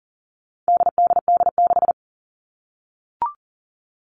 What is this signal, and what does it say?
Morse code, heard as a keyed tone, about 40 wpm, 700 Hz. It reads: DDD6
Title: All Cut Numbers - 4-Digit 40wpm